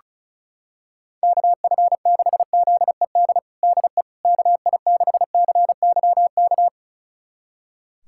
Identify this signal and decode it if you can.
Morse code, heard as a keyed tone, about 36 wpm, 700 Hz. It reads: KF6ZED DE KI6CYK
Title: USA Callsign DE Callsign - 6 Characters - 35wpm